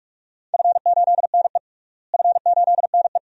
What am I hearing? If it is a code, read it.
Morse code, heard as a keyed tone, about 41 wpm, 700 Hz. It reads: F8NE F8NE
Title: DXCC-100 Call Signs - 4 Characters - Repeated 2x - 45wpm